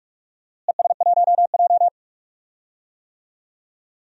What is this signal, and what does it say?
Morse code, heard as a keyed tone, about 44 wpm, 700 Hz. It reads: ES1J
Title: DXCC-100 Call Signs - 4 Characters - Repeated 2x - 45wpm